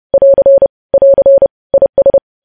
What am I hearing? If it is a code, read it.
Morse code, heard as a keyed tone, about 30 wpm, 550 Hz. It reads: <AR> <AR> IS
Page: news headlines in morse code at 30 wpm